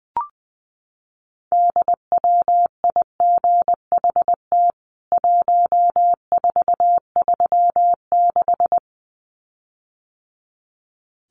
Morse code, heard as a keyed tone, about 20 wpm, 700 Hz. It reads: DWIGHT 1436
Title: CWT Exchange 20wpm